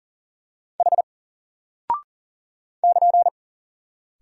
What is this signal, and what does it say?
Morse code, heard as a keyed tone, about 40 wpm, 700 Hz. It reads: H <KN>
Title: Single Character (Letter:Number:Punctuation:Prosign) Alt 40wpm